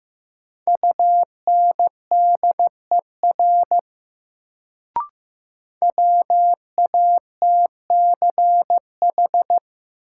Morse code, heard as a keyed tone, about 15 wpm, 700 Hz. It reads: UNDER WATCH
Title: Sentences from Top 300 Words - New Word Drill - 15wpm